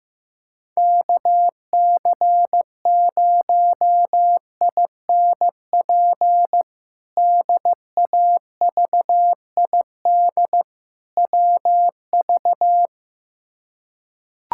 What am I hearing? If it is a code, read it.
Morse code, heard as a keyed tone, about 15 wpm, 700 Hz. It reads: KC0INP DAVID WV
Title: K1USN SST QSO Practice - 15wpm